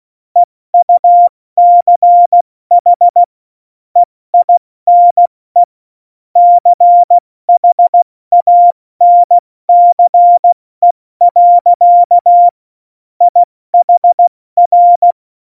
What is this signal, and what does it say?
Morse code, heard as a keyed tone, about 16 wpm, 700 Hz. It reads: EUCH EINE CHANCE. IHR
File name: Morsekurs / Witze / Witz 12. Januar 2021, 70 F80